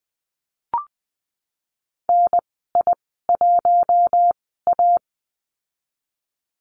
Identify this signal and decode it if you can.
Morse code, heard as a keyed tone, about 20 wpm, spaced as for 15 wpm, 700 Hz. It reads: NI1A
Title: 048 Lesson 9g - Number 4 - Call Signs 20wpm